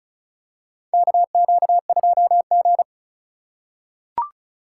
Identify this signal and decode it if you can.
Morse code, heard as a keyed tone, about 36 wpm, 700 Hz. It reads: KQ2G
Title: USA Call Signs - 4 Characters - 3x Character Spacing - 35wpm